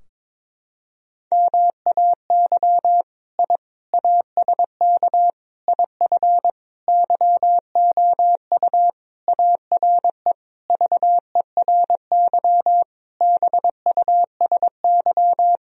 Morse code, heard as a keyed tone, about 21 wpm, 700 Hz. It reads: MAY I ASK IF YOU ARE VERY BUSY
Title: Sentences from Top 1000 Words - Review Part 3 - 22wpm